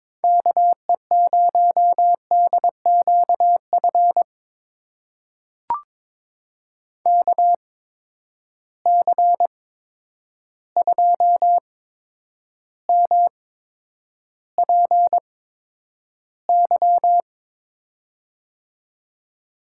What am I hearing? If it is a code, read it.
Morse code, heard as a keyed tone, about 22 wpm, spaced as for 9 wpm, 700 Hz. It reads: KE0DQF KC2MPY
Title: USA Call Signs - 6 Characters - 8x Character Spacing - 22wpm